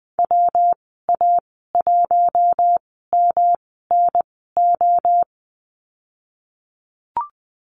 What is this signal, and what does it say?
Morse code, heard as a keyed tone, about 20 wpm, spaced as for 15 wpm, 700 Hz. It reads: WA1MNO
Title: 122 Lesson 20g - Number 3 - Call Signs 20wpm